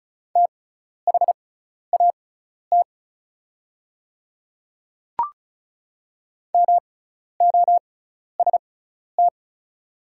Morse code, heard as a keyed tone, about 36 wpm, spaced as for 12 wpm, 700 Hz. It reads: THAT MOST
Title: 6x Character Spacing - Top 200 Words - 35wpm